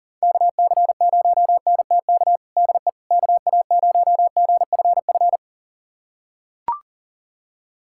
Morse code, heard as a keyed tone, about 40 wpm, 700 Hz. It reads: KC0NTK DE KA0GFF
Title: USA Callsign DE Callsign 40wpm